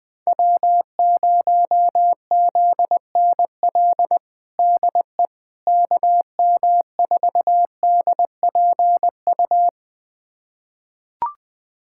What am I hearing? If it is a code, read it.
Morse code, heard as a keyed tone, about 20 wpm, 700 Hz. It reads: W0ZNL DE KM4DPU
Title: USA Callsign DE Callsign 20wpm